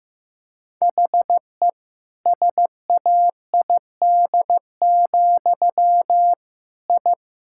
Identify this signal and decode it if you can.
Morse code, heard as a keyed tone, about 16 wpm, 700 Hz. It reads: HE SAID, I
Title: Sentences from Top 1000 Words - Review Part 6 - 15wpm